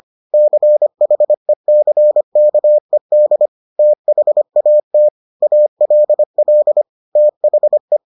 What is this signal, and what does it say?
Morse code, heard as a keyed tone, about 25 wpm, 600 Hz. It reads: CHECKED THAT ALL THE